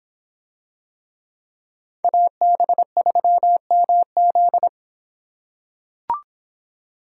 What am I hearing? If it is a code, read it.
Morse code, heard as a keyed tone, about 24 wpm, 700 Hz. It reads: AB3MZ